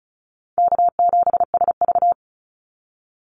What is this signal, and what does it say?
Morse code, heard as a keyed tone, about 34 wpm, 700 Hz. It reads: K7SV
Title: Easy Call Signs Repeated 2x - 35wpm